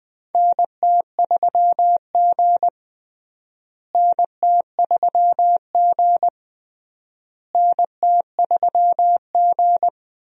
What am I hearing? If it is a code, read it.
Morse code, heard as a keyed tone, about 19 wpm, 700 Hz. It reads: NT3G NT3G NT3G
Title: USA Call Signs - 4 Characters - Repeated 3x - 20wpm